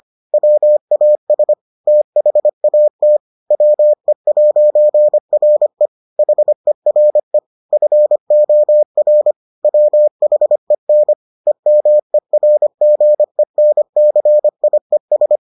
Morse code, heard as a keyed tone, about 25 wpm, 600 Hz. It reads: WAS THAT WE'RE HERE FOR WHEN EMERGENCIES